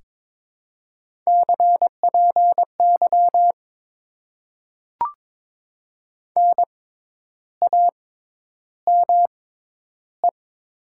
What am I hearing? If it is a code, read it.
Morse code, heard as a keyed tone, about 22 wpm, spaced as for 10 wpm, 700 Hz. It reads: CPY NAME
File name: Top 90 QSO Elements - 6x Character Spacing - 22wpm